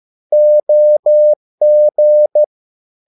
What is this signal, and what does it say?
Morse code, heard as a keyed tone, about 13 wpm, 600 Hz. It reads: OG
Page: Ferie 13 wpm